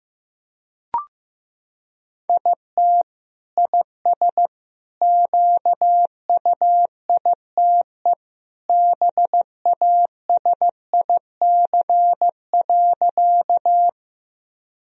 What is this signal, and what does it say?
Morse code, heard as a keyed tone, about 16 wpm, 700 Hz. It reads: IT IS QUITE BASIC.
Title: Sentences from Top 1000 Words - Review Part 3 - 15wpm